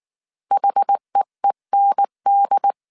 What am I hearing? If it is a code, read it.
Morse code, heard as a keyed tone, about 19 wpm, 800 Hz. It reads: HEEND